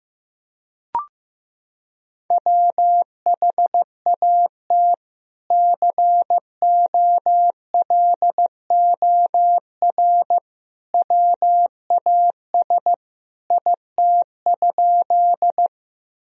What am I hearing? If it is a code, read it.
Morse code, heard as a keyed tone, about 15 wpm, 700 Hz. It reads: WHAT COLOR WAS IT?
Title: Sentences from Top 1000 Words - Review Part 2 - 15wpm